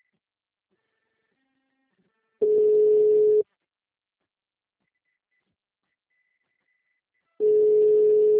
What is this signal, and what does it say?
Ringback tone (Europe, 425 Hz).
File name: out- -2021- - - .17231